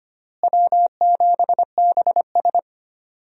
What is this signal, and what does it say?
Morse code, heard as a keyed tone, about 25 wpm, 700 Hz. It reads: W7BS
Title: USA Call Signs - 4 Characters - Repeated 2x - 25wpm